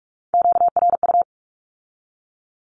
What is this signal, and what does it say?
Morse code, heard as a keyed tone, about 46 wpm, 700 Hz. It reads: QRU
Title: Q-Codes with Spoken Meaning - 45wpm